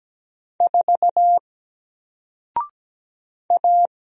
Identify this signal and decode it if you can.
Morse code, heard as a keyed tone, about 17 wpm, 700 Hz. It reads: 4 A
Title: Single Letter-Number Alt 17wpm